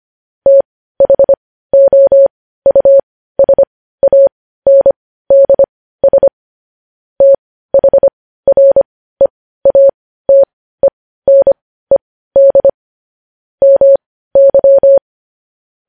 Morse code, heard as a keyed tone, about 25 wpm, spaced as for 15 wpm, 550 Hz. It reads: THOUSANDS THREATENED MY
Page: news headlines in morse code at 15 wpm